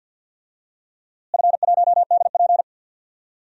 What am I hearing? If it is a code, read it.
Morse code, heard as a keyed tone, about 46 wpm, 700 Hz. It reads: F1DP
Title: DXCC-100 Call Signs - 4 Characters - Repeated 2x - 50wpm